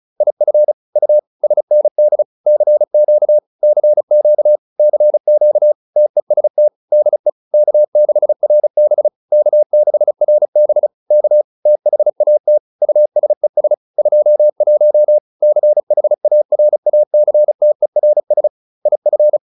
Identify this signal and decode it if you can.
Morse code, heard as a keyed tone, about 34 wpm, 600 Hz. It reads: IF U SND CQ CQ CQ TEST DE K6RB K6RB K THAT USES 21 CHARACTERS IF